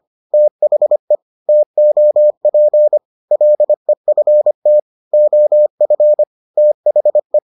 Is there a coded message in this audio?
Yes: Morse code from 0.3 to 7.4 s, heard as a keyed tone, about 25 wpm, 600 Hz. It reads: THE TOP LEFT OF THE